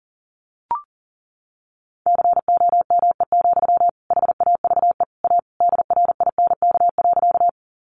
Morse code, heard as a keyed tone, about 40 wpm, 700 Hz. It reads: COME, HAVE A DRINK.
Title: Sentences from Top 1500 Words 40wpm